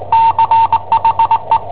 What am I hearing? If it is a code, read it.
Morse code, heard as a keyed tone, about 20 wpm, 910 Hz. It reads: CHE
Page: Cassetta esercizi preparata da I7OHP per IW5ELP